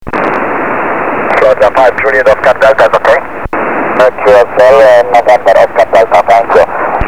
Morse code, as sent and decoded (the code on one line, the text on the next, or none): none